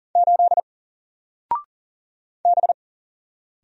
Morse code, heard as a keyed tone, about 40 wpm, 700 Hz. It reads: 8 B
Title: Single Letter-Number 40wpm